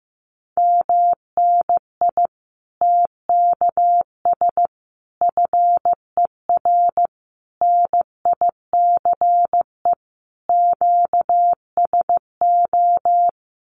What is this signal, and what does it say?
Morse code, heard as a keyed tone, about 15 wpm, 700 Hz. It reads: MNI TKS FER NICE QSO
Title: Common QSO Phrases 15wpm